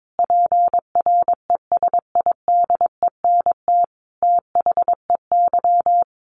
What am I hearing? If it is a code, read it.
Morse code, heard as a keyed tone, about 22 wpm, 700 Hz. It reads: PRESIDENT THEY
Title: A Sound Of Thunder 22wpm